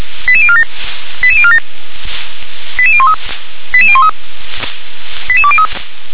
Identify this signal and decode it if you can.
Five-tone selective call (ZVEI), heard as quick runs of five tones, tones of about 70 ms. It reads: 88047 88047 88013 88013 88203
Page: SDR/ZVEI